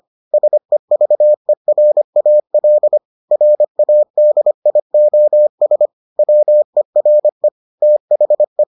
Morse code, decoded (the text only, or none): SEVERAL RADIOS WERE THE